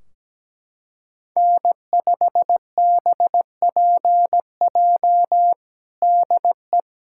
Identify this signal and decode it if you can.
Morse code, heard as a keyed tone, about 17 wpm, 700 Hz. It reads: N5BPJ DE
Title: K1USN SST QSO Practice 17wpm